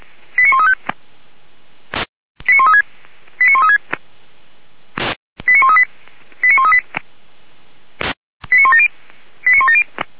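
Five-tone selective call (ZVEI), heard as quick runs of five tones, tones of about 70 ms. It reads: 89135 89136 89136 89138 89138 89170 89170
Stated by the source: ZVEI five-tone selective call received on an SDR.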